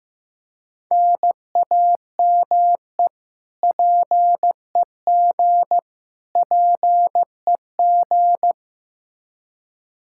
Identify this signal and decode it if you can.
Morse code, heard as a keyed tone, about 16 wpm, 700 Hz. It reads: NAME PEG PEG
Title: Practice QSO - Exchange 1 - Beginning - 15wpm